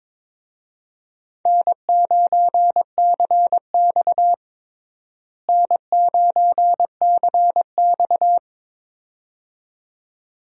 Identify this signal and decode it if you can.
Morse code, heard as a keyed tone, about 21 wpm, 700 Hz. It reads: N9CX N9CX
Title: USA Call Signs - 4 Characters - Repeated 2x - 22wpm